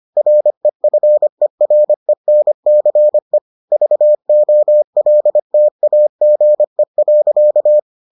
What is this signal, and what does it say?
Morse code, heard as a keyed tone, about 25 wpm, 600 Hz. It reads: REFERENCE VOLTAGE.